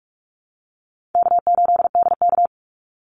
Morse code, heard as a keyed tone, about 44 wpm, 700 Hz. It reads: K8DK
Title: Easy Call Signs Repeated 3x - 45wpm